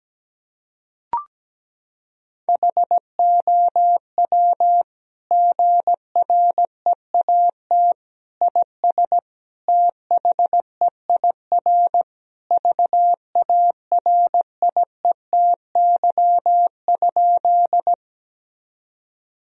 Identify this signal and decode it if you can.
Morse code, heard as a keyed tone, about 17 wpm, 700 Hz. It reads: HOW GREAT IS THEIR VARIETY?
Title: Sentences from Top 1000 Words - Review Part 4 - 17wpm